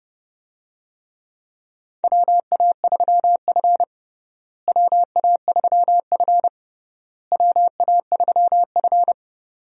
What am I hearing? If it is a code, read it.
Morse code, heard as a keyed tone, about 32 wpm, 700 Hz. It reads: WA3F WA3F WA3F
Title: USA Call Signs - 4 Characters - Repeated 3x - 30wpm